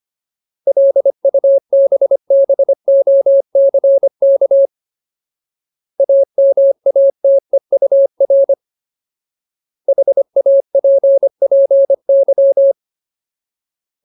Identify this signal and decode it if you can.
Morse code, heard as a keyed tone, about 25 wpm, 550 Hz. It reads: LUBBOCK AMATEUR HAPPY
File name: Ultimate Word List / 25WPM Ultimate